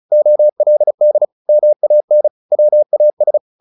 Morse code, heard as a keyed tone, about 34 wpm, 600 Hz. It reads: OLD MAN WAS
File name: the old man and the sea / the old man and the sea 1 35wpm